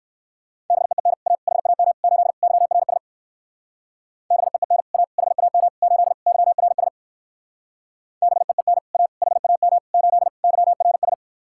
Morse code, heard as a keyed tone, about 57 wpm, 700 Hz. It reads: BEEN A HAM 8 YRS BEEN A HAM 8 YRS BEEN A HAM 8 YRS